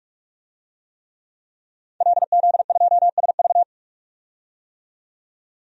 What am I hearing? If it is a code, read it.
Morse code, heard as a keyed tone, about 42 wpm, 700 Hz. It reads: LZ2SV
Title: DXCC-100 Call Signs - 5 Characters - 45wpm